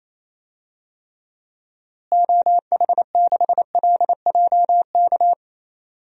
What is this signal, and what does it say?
Morse code, heard as a keyed tone, about 28 wpm, 700 Hz. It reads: OH6LJK
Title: DXCC-100 Call Signs - 6 Characters - Repeated 2x - 28wpm